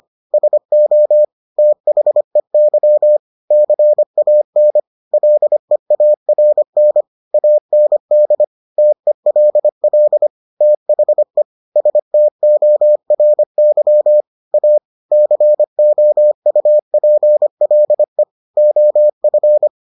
Morse code, heard as a keyed tone, about 25 wpm, 600 Hz. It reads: SO THEY CAN LEARN AND TELL THE STORY A COUPLE OF